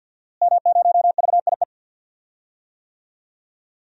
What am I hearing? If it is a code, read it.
Morse code, heard as a keyed tone, about 52 wpm, 700 Hz. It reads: M0VIE
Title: DXCC-100 Call Signs - 5 Characters - Repeated 2x - 50wpm